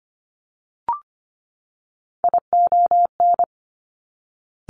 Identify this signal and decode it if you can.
Morse code, heard as a keyed tone, about 25 wpm, 700 Hz. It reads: ION
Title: Top Three Letter Combinations in Words 25wpm